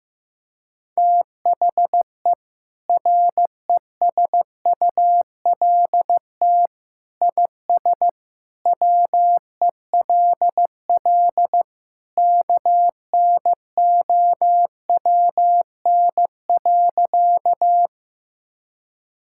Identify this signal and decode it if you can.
Morse code, heard as a keyed tone, about 15 wpm, 700 Hz. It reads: THE RESULT IS WELL KNOWN.
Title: Sentences from Top 1000 Words - Review Part 5 - 15wpm